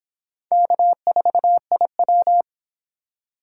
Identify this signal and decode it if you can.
Morse code, heard as a keyed tone, about 29 wpm, 700 Hz. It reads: K4IW